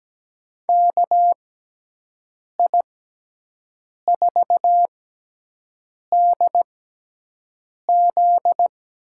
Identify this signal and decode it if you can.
Morse code, heard as a keyed tone, about 17 wpm, spaced as for 6 wpm, 700 Hz. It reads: KI4DZ
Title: USA Call Signs - 6 Characters - 6x Character Spacing - 17wpm